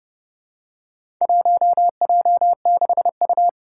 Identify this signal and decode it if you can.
Morse code, heard as a keyed tone, about 28 wpm, 700 Hz. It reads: 1J6U
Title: Four Challenging Letter-Number Combinations 30wpm